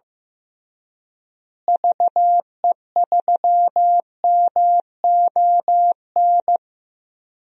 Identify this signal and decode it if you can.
Morse code, heard as a keyed tone, about 16 wpm, 700 Hz. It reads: VE3MON
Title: DXCC-100 Call Signs - 6 Characters - Repeated 3x - 15wpm